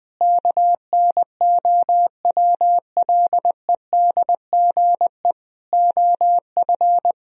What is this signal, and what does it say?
Morse code, heard as a keyed tone, about 20 wpm, 700 Hz. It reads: KNOWLEDGE OF
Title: Sentences from Top 1000 Words - Review Part 3 - 20wpm